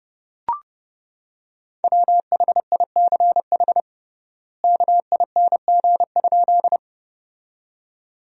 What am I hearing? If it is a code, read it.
Morse code, heard as a keyed tone, about 30 wpm, 700 Hz. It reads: WHICH KING?
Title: 3x Word-Spacing - Sentences from Top 400 Words - 30wpm